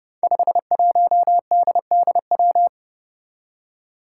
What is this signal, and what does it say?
Morse code, heard as a keyed tone, about 30 wpm, 700 Hz. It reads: 51DDW
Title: Code Groups - 5 Letter:Numbers - 30wpm